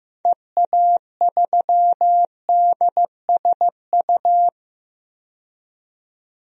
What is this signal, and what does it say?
Morse code, heard as a keyed tone, about 16 wpm, 700 Hz. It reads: EA3DSU
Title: DXCC-100 Call Signs - 6 Characters - 15wpm